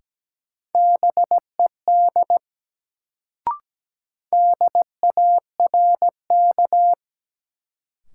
Morse code, heard as a keyed tone, about 17 wpm, 700 Hz. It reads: BED DARK
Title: Sentences from Top 500 Words - New Word Drill - 17wpm